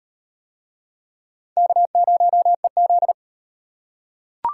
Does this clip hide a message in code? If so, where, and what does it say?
Yes: Morse code from 1.6 to 3.1 s, heard as a keyed tone, about 34 wpm, 700 Hz. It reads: K0EZ